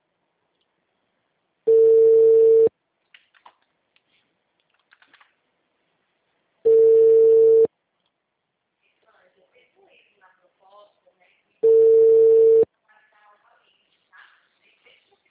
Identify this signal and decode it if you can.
Ringback tone (China, 450 Hz), with speech.